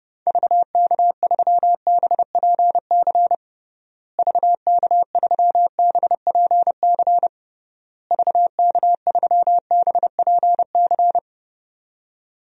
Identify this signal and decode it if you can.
Morse code, heard as a keyed tone, about 32 wpm, 700 Hz. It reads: VK3BPC VK3BPC VK3BPC
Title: DXCC-100 Call Signs - 6 Characters - Repeated 3x - 30wpm